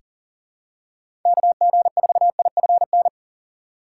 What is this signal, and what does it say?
Morse code, heard as a keyed tone, about 40 wpm, 700 Hz. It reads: KG4IFN
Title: USA Call Signs - 6 Characters - 2x Character Spacing - 40wpm